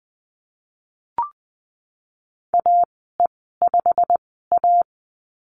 Morse code, heard as a keyed tone, about 20 wpm, spaced as for 14 wpm, 700 Hz. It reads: AE5A
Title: 088 Lesson 15g - Number 5 - Call Signs 20wpm